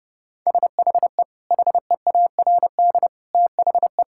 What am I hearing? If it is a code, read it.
Morse code, heard as a keyed tone, about 32 wpm, 700 Hz. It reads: SHE HEARD THE
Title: Sentences from Top 1600 Words 30wpm